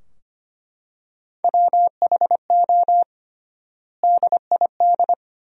Morse code, heard as a keyed tone, about 25 wpm, 700 Hz. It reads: WHO DID
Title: 3x Word-Spacing - Sentences from Top 300 Words - 25wpm